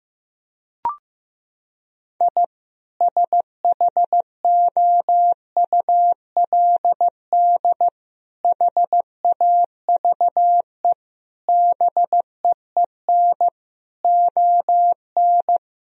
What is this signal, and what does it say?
Morse code, heard as a keyed tone, about 15 wpm, 700 Hz. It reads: I SHOULD HAVE BEEN ON
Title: Sentences from Top 1000 Words - Review Part 1 - 15wpm